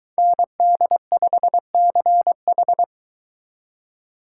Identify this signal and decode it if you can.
Morse code, heard as a keyed tone, about 23 wpm, 700 Hz. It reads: ND5CH